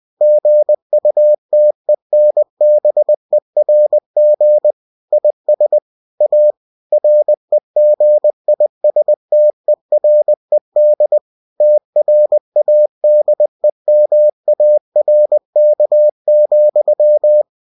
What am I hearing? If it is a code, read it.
Morse code, heard as a keyed tone, about 20 wpm, 600 Hz. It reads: GUTENBERG IS A REGISTERED TRADEMARK,
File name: cw-books / winnie-the-pooh / winnie-20wpm-page-0112